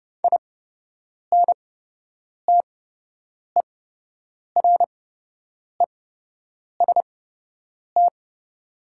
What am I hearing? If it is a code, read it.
Morse code, heard as a keyed tone, about 30 wpm, spaced as for 8 wpm, 700 Hz. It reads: INTEREST
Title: 8x Character Spacing - Top 500 Words - 30wpm